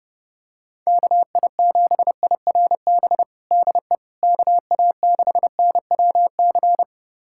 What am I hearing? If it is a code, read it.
Morse code, heard as a keyed tone, about 30 wpm, 700 Hz. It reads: KI7IRB DE KA6NWC
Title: USA Callsign DE Callsign - 6 Characters - 30wpm